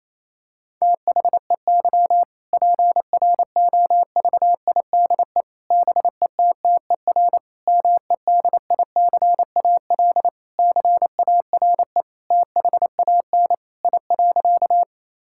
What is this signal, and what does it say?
Morse code, heard as a keyed tone, about 28 wpm, 700 Hz. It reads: THEY PROVIDE BETTER MEDICAL CARE THAN I.
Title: Sentences from Top 1000 Words - Review Part 5 - 28wpm